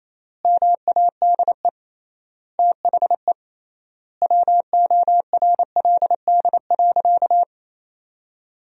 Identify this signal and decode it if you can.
Morse code, heard as a keyed tone, about 28 wpm, 700 Hz. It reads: MADE THE WORLD.
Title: 3x Word-Spacing - Sentences from Top 200 Words - 28wpm